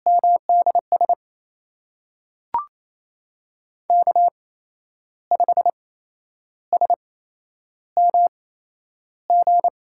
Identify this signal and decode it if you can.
Morse code, heard as a keyed tone, about 28 wpm, spaced as for 10 wpm, 700 Hz. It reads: MDS K5SMG
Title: USA Call Signs - 5 Characters - 8x Character Spacing - 28wpm